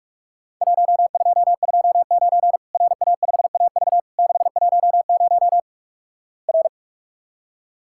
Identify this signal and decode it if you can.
Morse code, heard as a keyed tone, about 48 wpm, 700 Hz. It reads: 1229 RA5AV 610 R
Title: Worked All Europe DX Contest - QTC Exchange - 45wpm